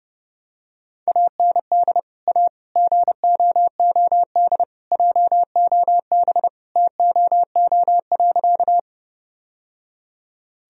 Morse code, heard as a keyed tone, about 30 wpm, 700 Hz. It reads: AND A GOOD JOB TOO.
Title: Sentences from Top 1000 Words - Review Part 3 - 30wpm